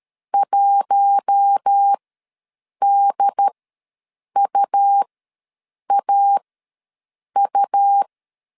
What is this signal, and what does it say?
Morse code, heard as a keyed tone, about 13 wpm, spaced as for 7 wpm, 800 Hz. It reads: 1DUAU
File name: LEZIONI / 6 PKX0 / 6 P ws 12wpm 3117